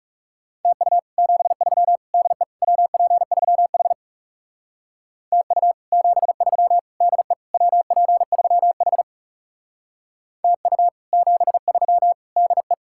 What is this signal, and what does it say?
Morse code, heard as a keyed tone, about 37 wpm, 700 Hz. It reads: TU 73 DE WP3H TU 73 DE WP3H TU 73 DE
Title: Speed-Racing Common QSO Phrases - 30wpm